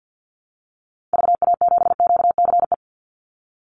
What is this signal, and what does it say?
Morse code, heard as a keyed tone, about 50 wpm, 700 Hz. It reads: VA7QCE
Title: Call Signs Repeated 3x - 50wpm